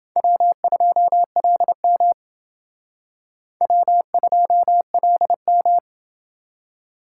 Morse code, heard as a keyed tone, about 29 wpm, 700 Hz. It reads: W2LM W2LM